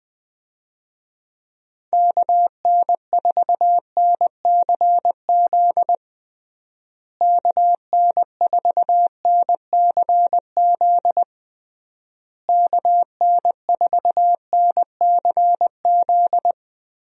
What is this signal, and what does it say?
Morse code, heard as a keyed tone, about 20 wpm, 700 Hz. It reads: KN4NCZ KN4NCZ KN4NCZ
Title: USA Call Signs - 6 Characters - Repeated 3x - 20wpm